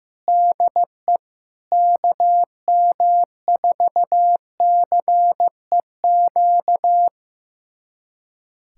Morse code, heard as a keyed tone, about 16 wpm, 700 Hz. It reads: DE KM4CEQ
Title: USA Callsign DE Callsign - 6 Characters - 15wpm